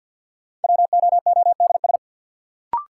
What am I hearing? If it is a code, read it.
Morse code, heard as a keyed tone, about 52 wpm, 700 Hz. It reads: WOODS
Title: Sentences from Top 1000 Words - Review Part 2 - New Word Drill - 50wpm